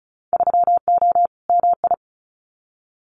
Morse code, heard as a keyed tone, about 34 wpm, 700 Hz. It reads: 3O MI
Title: Winter Field Day Exchanges 35wpm